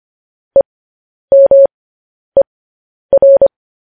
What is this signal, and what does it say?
Morse code, heard as a keyed tone, about 25 wpm, spaced as for 10 wpm, 550 Hz. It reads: EMER
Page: news headlines in morse code at 10 wpm